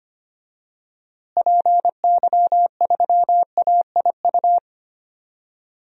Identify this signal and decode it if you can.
Morse code, heard as a keyed tone, about 25 wpm, 700 Hz. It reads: PY3AIU
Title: DXCC-100 Call Signs - 6 Characters - Repeated 3x - 25wpm